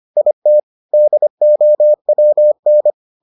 Morse code, heard as a keyed tone, about 25 wpm, 600 Hz. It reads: IT DOWN